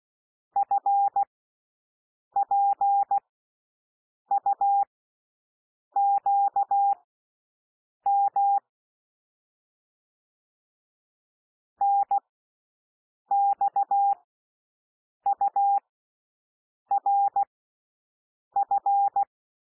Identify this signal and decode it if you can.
Morse code, heard as a keyed tone, about 16 wpm, spaced as for 6 wpm, 800 Hz. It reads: FPUQM NXURF
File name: cwmp3 / L1 SOUND LETTERS 16 wpm